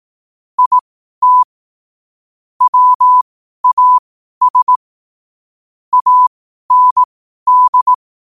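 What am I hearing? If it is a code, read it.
Morse code, heard as a keyed tone, about 18 wpm, spaced as for 12 wpm, 1000 Hz. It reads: IT WAS AND
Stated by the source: Morse code sent at 12 words per minute.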